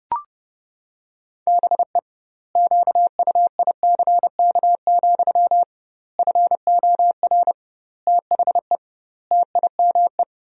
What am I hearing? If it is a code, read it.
Morse code, heard as a keyed tone, about 30 wpm, 700 Hz. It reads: BE QUICK, FOR THE TIME
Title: 2x Word-Spacing - Sentences from Top 500 Words - 30wpm